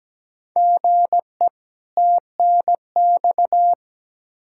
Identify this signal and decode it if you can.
Morse code, heard as a keyed tone, about 17 wpm, 700 Hz. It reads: GE TNX
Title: Practice QSO - Exchange 1 - Beginning - 17wpm